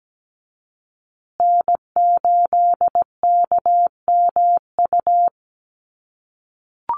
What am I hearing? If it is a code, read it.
Morse code, heard as a keyed tone, about 17 wpm, 700 Hz. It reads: N8KMU
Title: USA Call Signs Repeated 2x 17wpm